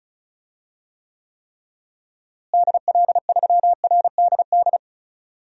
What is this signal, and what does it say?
Morse code, heard as a keyed tone, about 34 wpm, 700 Hz. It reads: DL3RDD
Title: DXCC-100 Call Signs - 6 Characters - Repeated 3x - 35wpm